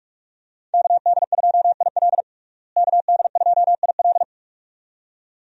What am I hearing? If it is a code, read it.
Morse code, heard as a keyed tone, about 46 wpm, 700 Hz. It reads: KD2IL KD2IL
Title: USA Call Signs - 5 Characters - Repeated 2x - 45wpm